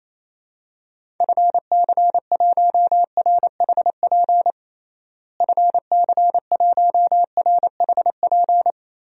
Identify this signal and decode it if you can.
Morse code, heard as a keyed tone, about 28 wpm, 700 Hz. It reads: FC1RHP FC1RHP
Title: DXCC-100 Call Signs - 6 Characters - Repeated 3x - 28wpm